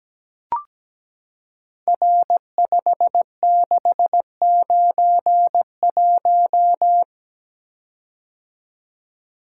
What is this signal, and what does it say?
Morse code, heard as a keyed tone, about 17 wpm, 700 Hz. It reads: R5691
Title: Code Groups - 5 Letter:Numbers - 17wpm